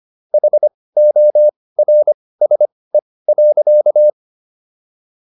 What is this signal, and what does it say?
Morse code, heard as a keyed tone, about 25 wpm, spaced as for 18 wpm, 600 Hz. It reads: HORSE.
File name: SS123 18